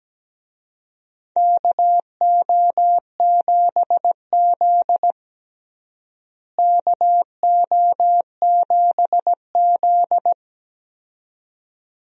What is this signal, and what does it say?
Morse code, heard as a keyed tone, about 17 wpm, 700 Hz. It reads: KO7Z KO7Z
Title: USA Call Signs - 4 Characters - Repeated 2x - 17wpm